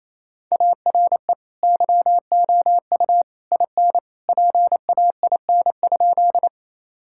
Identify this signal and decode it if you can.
Morse code, heard as a keyed tone, about 28 wpm, 700 Hz. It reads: ARE YOU IN PAIN?
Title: Sentences from Top 1600 Words 28wpm